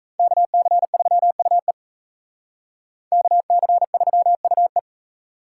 Morse code, heard as a keyed tone, about 36 wpm, 700 Hz. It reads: KC3UE KC3UE